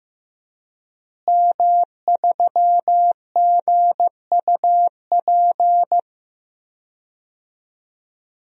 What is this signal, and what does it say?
Morse code, heard as a keyed tone, about 15 wpm, 700 Hz. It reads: M3GUP
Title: DXCC-100 Call Signs - 5 Characters - Repeated 2x - 15wpm